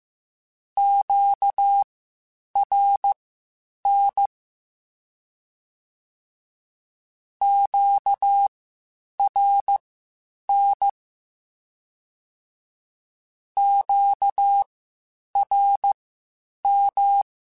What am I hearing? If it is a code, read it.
Morse code, heard as a keyed tone, about 15 wpm, spaced as for 8 wpm, 800 Hz. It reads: QRN QRN QRM
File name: (spaced) / Q-Calls Spaced-mp3s / Q Calls Twice sp @13